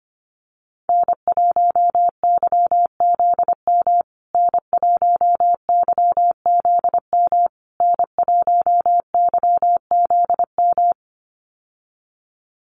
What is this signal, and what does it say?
Morse code, heard as a keyed tone, about 25 wpm, 700 Hz. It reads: N1YZM N1YZM N1YZM
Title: USA Call Signs Repeated 3x 25wpm